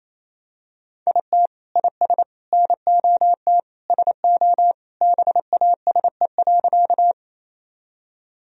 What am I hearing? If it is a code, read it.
Morse code, heard as a keyed tone, about 28 wpm, 700 Hz. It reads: IT IS NOT SO BASE.
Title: Sentences from Top 1000 Words - Review Part 1 - 28wpm